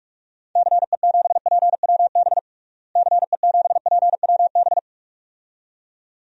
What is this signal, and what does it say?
Morse code, heard as a keyed tone, about 46 wpm, 700 Hz. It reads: CE7PWB CE7PWB
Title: DXCC-100 Call Signs - 6 Characters - Repeated 2x - 45wpm